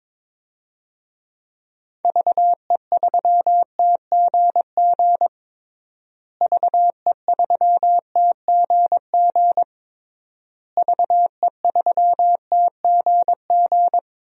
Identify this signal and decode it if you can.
Morse code, heard as a keyed tone, about 22 wpm, 700 Hz. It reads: VE3TGG VE3TGG VE3TGG
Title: DXCC-100 Call Signs - 6 Characters - Repeated 3x - 22wpm